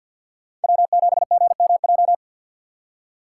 Morse code, heard as a keyed tone, about 54 wpm, 700 Hz. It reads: W7GMJ
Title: USA Call Signs - 5 Characters - 6x Character Spacing - 50wpm